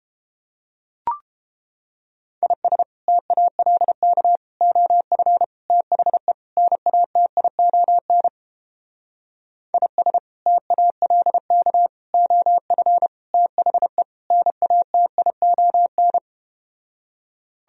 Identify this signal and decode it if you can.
Morse code, heard as a keyed tone, about 30 wpm, 700 Hz. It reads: IS TALK OF THE NATION IS TALK OF THE NATION